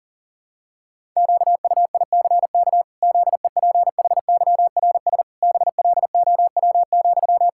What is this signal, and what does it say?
Morse code, heard as a keyed tone, about 37 wpm, 700 Hz. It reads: QUICK ZEPHYRS BLOW,
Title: Pangram Challenge v2 - 40wpm